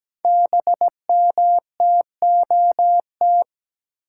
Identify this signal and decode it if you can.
Morse code, heard as a keyed tone, about 17 wpm, 700 Hz. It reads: BMTOT
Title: Code Groups - 5 Letters - 17wpm